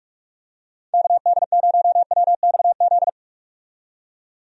Morse code, heard as a keyed tone, about 46 wpm, 700 Hz. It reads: KD0WXZ
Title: USA Call Signs - 6 Characters - 45wpm